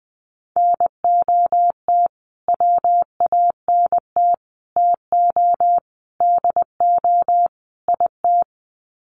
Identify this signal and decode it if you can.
Morse code, heard as a keyed tone, about 20 wpm, 700 Hz. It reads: NOT WANT TO DO IT
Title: Sets of 5 Words Encore 20wpm